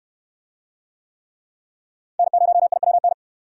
Morse code, heard as a keyed tone, about 69 wpm, 700 Hz. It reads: N0IWN